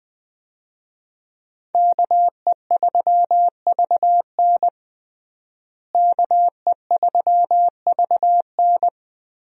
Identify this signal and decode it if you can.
Morse code, heard as a keyed tone, about 21 wpm, 700 Hz. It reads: KE3VN KE3VN
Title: USA Call Signs - 5 Characters - Repeated 3x - 20wpm